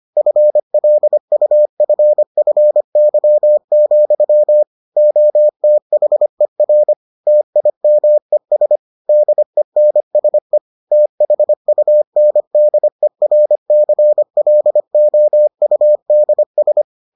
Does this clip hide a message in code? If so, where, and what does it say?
Yes: Morse code from 0.2 to 16.8 s, heard as a keyed tone, about 25 wpm, 600 Hz. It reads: FLUFFY, OTHER TIMES DENSE THUNDERCLOUDS